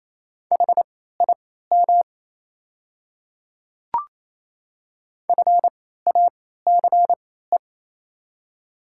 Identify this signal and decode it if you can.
Morse code, heard as a keyed tone, about 27 wpm, spaced as for 16 wpm, 700 Hz. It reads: HIM FACE
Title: 3x Character Spacing - Top 300 Words - 28wpm